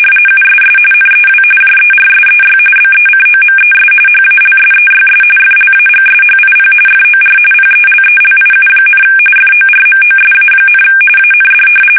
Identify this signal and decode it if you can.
RTTY, about 45.45 baud. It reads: USS NEW JERSEY BECOMES HALSEYS FLAGSHIP FOR PELELIU, FORMOSA, AND
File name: fsk850 / battleshipnj